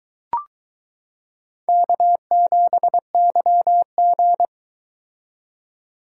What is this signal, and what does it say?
Morse code, heard as a keyed tone, about 23 wpm, 700 Hz. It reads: K7YG